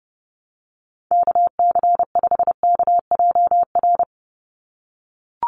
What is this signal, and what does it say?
Morse code, heard as a keyed tone, about 30 wpm, 700 Hz. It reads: KC5KJR
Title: USA Call Signs 30wpm